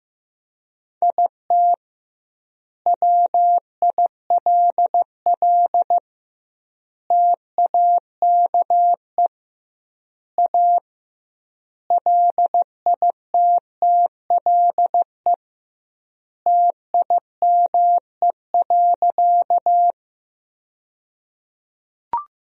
Morse code, heard as a keyed tone, about 16 wpm, 700 Hz. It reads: IT WILL TAKE A LITTLE TIME.
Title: 2x Word-Spacing - Sentences from Top 200 Words - 15wpm